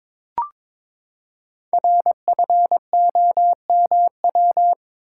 Morse code, heard as a keyed tone, about 22 wpm, 700 Hz. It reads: RFOMW
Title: Code Groups - 5 Letters - 22wpm